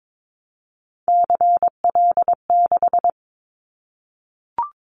Morse code, heard as a keyed tone, about 22 wpm, 700 Hz. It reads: CL6
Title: Callsign Prefixes for Top 100 DXCC - 22wpm